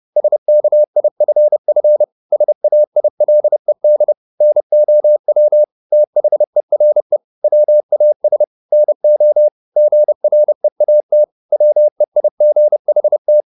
Morse code, heard as a keyed tone, about 30 wpm, 600 Hz. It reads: SKIFF SAILED NOW THERE WAS NO GREAT WEIGHT